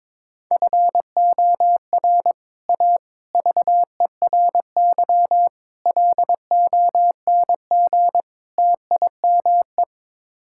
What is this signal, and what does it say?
Morse code, heard as a keyed tone, about 21 wpm, 700 Hz. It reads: FOR A VERY LONG TIME
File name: practice / 22wpm / Build A Phrase - 5-Word Phrases - 22wpm